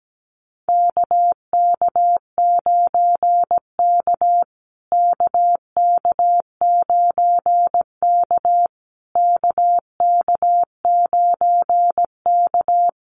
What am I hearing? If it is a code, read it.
Morse code, heard as a keyed tone, about 17 wpm, 700 Hz. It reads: KK9K KK9K KK9K
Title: USA Call Signs Repeated 3x 17wpm